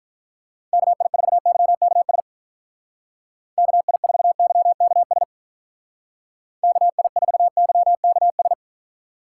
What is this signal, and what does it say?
Morse code, heard as a keyed tone, about 45 wpm, 700 Hz. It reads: KI4YKS KI4YKS KI4YKS